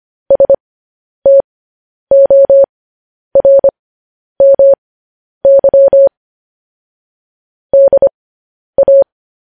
Morse code, heard as a keyed tone, about 25 wpm, spaced as for 10 wpm, 550 Hz. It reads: STORMY DA
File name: podcast / News headlines-1x-25x10